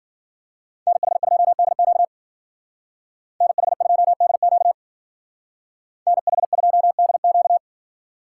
Morse code, heard as a keyed tone, about 52 wpm, 700 Hz. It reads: NH2DQ NH2DQ NH2DQ